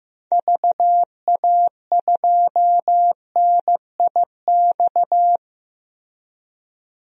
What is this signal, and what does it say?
Morse code, heard as a keyed tone, about 15 wpm, 700 Hz. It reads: VA2NIX
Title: DXCC-100 Call Signs - 6 Characters - 15wpm